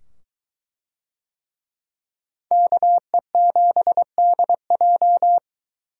Morse code, heard as a keyed tone, about 23 wpm, 700 Hz. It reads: KE7DJ